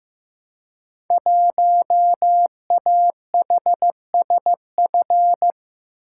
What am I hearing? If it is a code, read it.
Morse code, heard as a keyed tone, about 15 wpm, 700 Hz. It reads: 1AHSF
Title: Code Groups - 5 Letter:Numbers - 15wpm